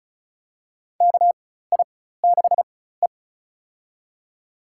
Morse code, heard as a keyed tone, about 36 wpm, spaced as for 16 wpm, 700 Hz. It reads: KI6E
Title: USA Call Signs - 4 Characters - 4x Character Spacing - 35wpm